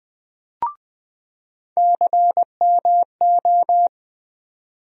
Morse code, heard as a keyed tone, about 20 wpm, 700 Hz. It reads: CMO
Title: USA Call Sign Postfixes - 3 Letters 20wpm